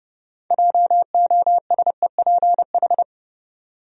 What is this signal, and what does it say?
Morse code, heard as a keyed tone, about 30 wpm, 700 Hz. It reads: JOSEPH
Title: USA Common Names - 30wpm